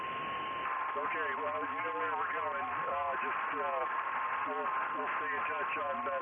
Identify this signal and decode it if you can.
FT8, with speech.